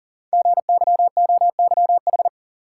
Morse code, heard as a keyed tone, about 41 wpm, 700 Hz. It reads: GYOYH
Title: Code Groups - 5 Letter:Numbers - 40wpm